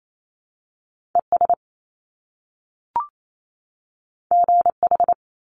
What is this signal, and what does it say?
Morse code, heard as a keyed tone, about 28 wpm, 700 Hz. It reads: ES GH
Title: Top Two Letter Combinations in Words 28wpm